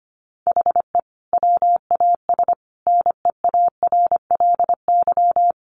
Morse code, heard as a keyed tone, about 25 wpm, 700 Hz. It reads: HE WAS NEARLY
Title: Sentences from Top 1300 Words 25wpm